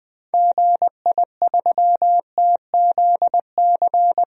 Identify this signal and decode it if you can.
Morse code, heard as a keyed tone, about 21 wpm, 700 Hz. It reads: GI3TZC
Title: DXCC-100 Call Signs - 6 Characters - 20wpm